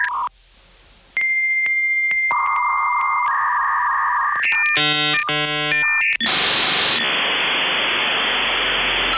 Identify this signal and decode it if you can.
Dial-up modem handshake.